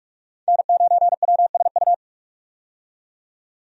Morse code, heard as a keyed tone, about 43 wpm, 700 Hz. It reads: N9WSU
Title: USA Call Signs - 5 Characters - 45wpm